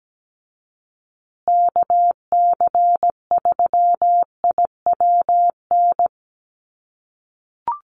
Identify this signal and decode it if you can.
Morse code, heard as a keyed tone, about 17 wpm, 700 Hz. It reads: KC3IWN
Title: USA Call Signs Repeated 2x 17wpm